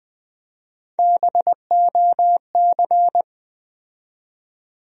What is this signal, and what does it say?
Morse code, heard as a keyed tone, about 20 wpm, 700 Hz. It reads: BOC
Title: USA Call Sign Postfixes - 3 Letters 20wpm